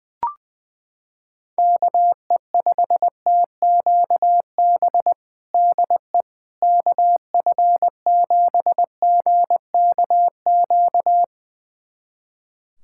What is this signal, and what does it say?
Morse code, heard as a keyed tone, about 21 wpm, 700 Hz. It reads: KE5TQB DE KF7GKQ
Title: USA Callsign DE Callsign - 6 Characters - 20wpm